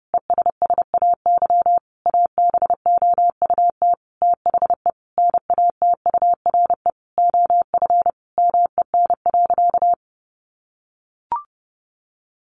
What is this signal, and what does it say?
Morse code, heard as a keyed tone, about 30 wpm, 700 Hz. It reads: ESSAY ABOUT THE NATURE OF MEN.
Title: Sentences from Top 1500 Words 30wpm